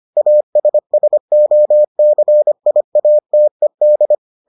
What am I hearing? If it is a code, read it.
Morse code, heard as a keyed tone, about 25 wpm, 600 Hz. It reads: ASSOCIATED